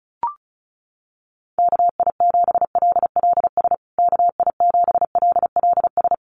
Morse code, heard as a keyed tone, about 34 wpm, 700 Hz. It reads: KI7LLS KI7LLS
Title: USA Call Signs Repeated 3x 35wpm